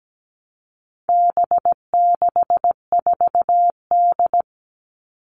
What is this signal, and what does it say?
Morse code, heard as a keyed tone, about 17 wpm, 700 Hz. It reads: B64D
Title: All Cut Numbers - 4-Digit 17wpm